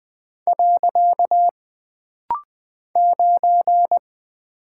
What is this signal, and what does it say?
Morse code, heard as a keyed tone, about 20 wpm, 700 Hz. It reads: . 9
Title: Single Character (Letter:Number:Punctuation:Prosign) - Rapid-Fire - 20wpm